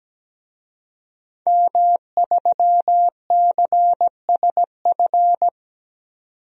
Morse code, heard as a keyed tone, about 17 wpm, 700 Hz. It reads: M3CSF
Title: DXCC-100 Call Signs - 5 Characters - 17wpm